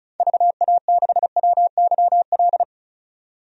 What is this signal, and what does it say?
Morse code, heard as a keyed tone, about 36 wpm, 700 Hz. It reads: VA6WYL
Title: DXCC-100 Call Signs - 6 Characters - 35wpm